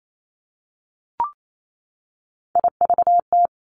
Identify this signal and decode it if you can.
Morse code, heard as a keyed tone, about 28 wpm, 700 Hz. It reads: IVT
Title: Three Challenging Letter-Number Combinations 28wpm